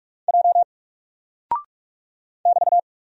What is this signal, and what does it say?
Morse code, heard as a keyed tone, about 44 wpm, 700 Hz. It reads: J <BT>
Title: Single Character (Letter:Number:Punctuation:Prosign) Alt 45wpm